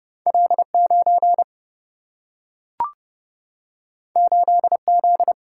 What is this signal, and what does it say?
Morse code, heard as a keyed tone, about 31 wpm, 700 Hz. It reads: L 9 8 Z
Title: Single-Letter Single-Number 30wpm